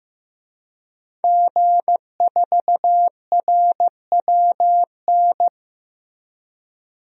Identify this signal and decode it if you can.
Morse code, heard as a keyed tone, about 15 wpm, 700 Hz. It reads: G4RWN
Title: DXCC-100 Call Signs - 5 Characters - 15wpm